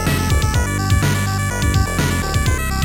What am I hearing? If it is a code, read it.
There is no code.